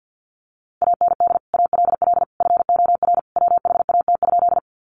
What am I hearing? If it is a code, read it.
Morse code, heard as a keyed tone, about 50 wpm, 700 Hz. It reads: AND ALL FOR WHAT?
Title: Sentences from Top 100 Words 50wpm